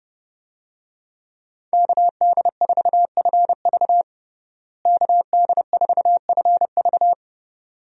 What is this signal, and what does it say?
Morse code, heard as a keyed tone, about 31 wpm, 700 Hz. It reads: KD4FV KD4FV
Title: USA Call Signs - 5 Characters - Repeated 2x - 30wpm